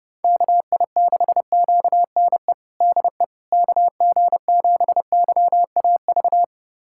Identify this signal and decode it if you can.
Morse code, heard as a keyed tone, about 32 wpm, 700 Hz. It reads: KI6QNE DE KG7YAV
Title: USA Callsign DE Callsign 30wpm